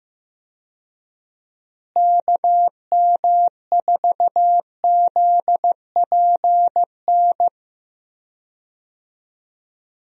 Morse code, heard as a keyed tone, about 15 wpm, 700 Hz. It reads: KM4ZPN
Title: USA Call Signs - 6 Characters - Repeated 2x - 15wpm